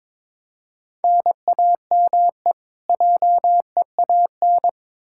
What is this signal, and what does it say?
Morse code, heard as a keyed tone, about 22 wpm, 700 Hz. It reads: NAME JEAN
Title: Practice QSO - Exchange 1 - Beginning - 22wpm